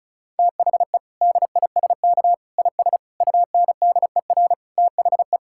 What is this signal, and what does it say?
Morse code, heard as a keyed tone, about 34 wpm, 700 Hz. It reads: THE DISK IS UNDER THE
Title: Sentences from Top 1000 Words - Review Part 6 - 35wpm